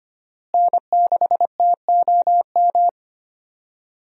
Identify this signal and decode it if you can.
Morse code, heard as a keyed tone, about 25 wpm, 700 Hz. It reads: N6TOM
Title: USA Call Signs - 5 Characters - Repeated 2x - 25wpm